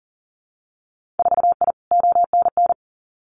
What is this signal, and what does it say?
Morse code, heard as a keyed tone, about 42 wpm, 700 Hz. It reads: 4I ONN
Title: Winter Field Day Exchanges Encore 40wpm